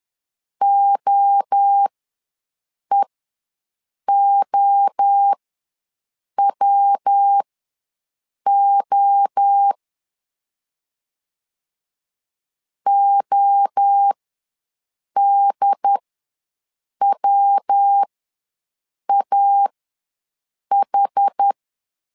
Morse code, heard as a keyed tone, about 10 wpm, spaced as for 6 wpm, 800 Hz. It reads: OEOWO ODWAH
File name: LEZIONI / 3 MWRO / 3 MWRO ws 10wpm 2310